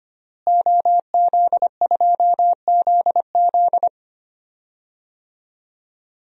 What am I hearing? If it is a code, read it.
Morse code, heard as a keyed tone, about 25 wpm, 700 Hz. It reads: OZ2ZZ
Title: DXCC-100 Call Signs - 5 Characters - Repeated 2x - 25wpm